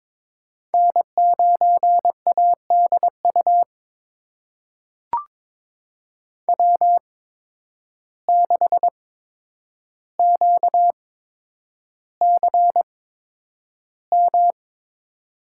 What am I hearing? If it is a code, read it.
Morse code, heard as a keyed tone, about 22 wpm, spaced as for 9 wpm, 700 Hz. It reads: N9ADU W6QCM
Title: USA Call Signs - 5 Characters - 8x Character Spacing - 22wpm